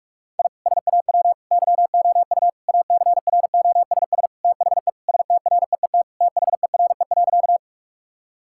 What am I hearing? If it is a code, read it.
Morse code, heard as a keyed tone, about 41 wpm, 700 Hz. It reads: I SAW YOU ACROSS THE STREET THERE.
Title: Sentences from Top 1000 Words - Review Part 5 - 45wpm